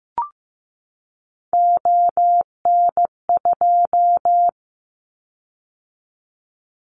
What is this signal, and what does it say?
Morse code, heard as a keyed tone, about 15 wpm, 700 Hz. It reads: ON2
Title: Callsign Prefixes for Top 50 DXCC - 15wpm